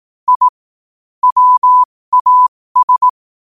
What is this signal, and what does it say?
Morse code, heard as a keyed tone, about 18 wpm, spaced as for 15 wpm, 1000 Hz. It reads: I WAS
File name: qotdfiles / 15wpm